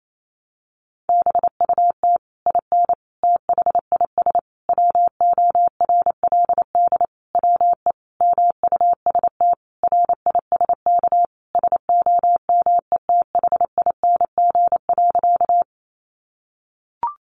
Morse code, heard as a keyed tone, about 28 wpm, 700 Hz. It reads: BUT IN THIS WORLD WE MUST RISK SOMETHING.
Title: Sentences from Top 600 Words 28wpm